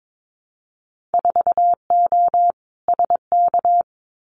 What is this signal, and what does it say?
Morse code, heard as a keyed tone, about 22 wpm, 700 Hz. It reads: 4O SK
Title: Winter Field Day Exchanges Encore 22wpm